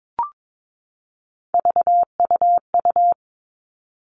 Morse code, heard as a keyed tone, about 22 wpm, 700 Hz. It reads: VUU
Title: All Cut Numbers - 3-Digit 22wpm